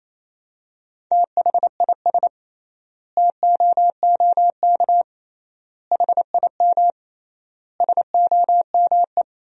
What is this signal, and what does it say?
Morse code, heard as a keyed tone, about 28 wpm, 700 Hz. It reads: THIS TOOK HIM SOME
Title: 3x Word-Spacing - Sentences from Top 300 Words - 28wpm